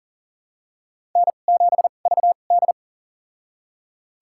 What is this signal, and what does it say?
Morse code, heard as a keyed tone, about 40 wpm, spaced as for 29 wpm, 690 Hz. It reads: N7VD
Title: USA Call Signs - 4 Characters - 2x Character Spacing - 40wpm